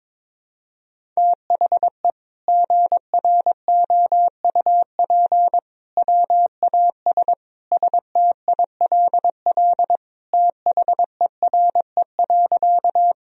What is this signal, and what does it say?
Morse code, heard as a keyed tone, about 22 wpm, 700 Hz. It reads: THE GROUP WAS STILL THERE.
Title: Sentences from Top 1000 Words - Review Part 1 - 22wpm